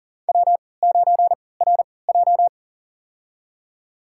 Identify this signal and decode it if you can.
Morse code, heard as a keyed tone, about 40 wpm, spaced as for 23 wpm, 700 Hz. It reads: W9RJ
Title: USA Call Signs - 3x Character Spacing - 40wpm